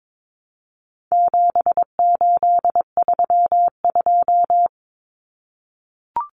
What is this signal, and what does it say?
Morse code, heard as a keyed tone, about 22 wpm, 700 Hz. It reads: 7832
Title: Common Cut Numbers - 4-Digit 22wpm